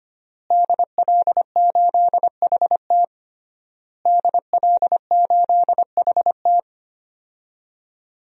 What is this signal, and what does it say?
Morse code, heard as a keyed tone, about 25 wpm, 700 Hz. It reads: DL8HT DL8HT
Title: DXCC-100 Call Signs - 5 Characters - Repeated 3x - 25wpm